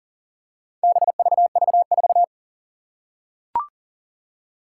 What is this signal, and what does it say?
Morse code, heard as a keyed tone, about 43 wpm, 700 Hz. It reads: BVV4
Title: Four Challenging Letter-Number Combinations 40wpm